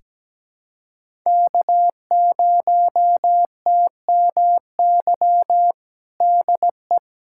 Morse code, heard as a keyed tone, about 17 wpm, 700 Hz. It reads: K0TMY DE
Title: USA Callsign DE Callsign 17wpm